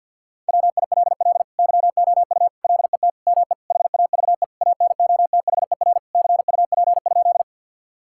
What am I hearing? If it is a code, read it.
Morse code, heard as a keyed tone, about 50 wpm, 700 Hz. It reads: WILL YOU LET ME HAVE ANOTHER CUP?
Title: Sentences from Top 1600 Words 50wpm